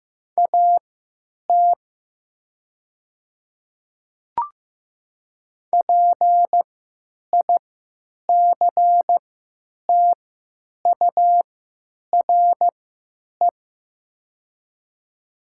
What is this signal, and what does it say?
Morse code, heard as a keyed tone, about 15 wpm, spaced as for 9 wpm, 700 Hz. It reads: AT PICTURE
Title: 3x Character Spacing - Top 200 Words - 15wpm